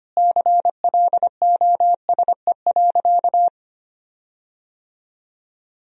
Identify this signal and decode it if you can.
Morse code, heard as a keyed tone, about 25 wpm, 700 Hz. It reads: CLOSE.
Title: 4x Word-Spacing - Sentences from Top 300 Words - 25wpm